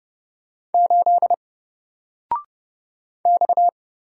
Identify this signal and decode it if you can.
Morse code, heard as a keyed tone, about 30 wpm, 700 Hz. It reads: 8 X
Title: Single Letter-Number 30wpm